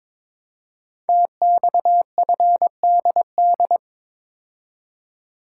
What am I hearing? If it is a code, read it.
Morse code, heard as a keyed tone, about 22 wpm, 700 Hz. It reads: TXFDD
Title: Code Groups - 5 Letters - 22wpm